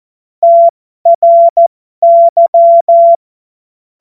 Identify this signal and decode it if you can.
Morse code, heard as a keyed tone, about 14 wpm, spaced as for 12 wpm, 680 Hz. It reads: TRY
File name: Morsekurs / Witze